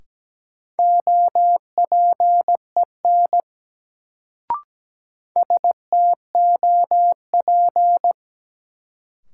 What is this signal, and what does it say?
Morse code, heard as a keyed tone, about 17 wpm, 700 Hz. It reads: OPEN STOP
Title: Sentences from Top 300 Words - New Word Drill - 17wpm